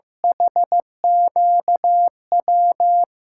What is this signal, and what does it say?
Morse code, heard as a keyed tone, about 15 wpm, 700 Hz. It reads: HQW
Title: USA Call Signs - 6 Characters - 6x Character Spacing - 15wpm